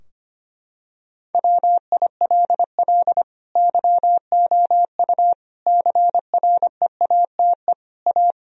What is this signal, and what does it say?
Morse code, heard as a keyed tone, about 25 wpm, 700 Hz. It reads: WILL YOU CREATE A
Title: Sentences from Top 1000 Words - Review Part 3 - 25wpm